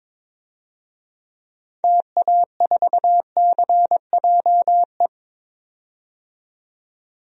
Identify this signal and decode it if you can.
Morse code, heard as a keyed tone, about 21 wpm, 700 Hz. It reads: TA4CJE
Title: DXCC-100 Call Signs - 6 Characters - 22wpm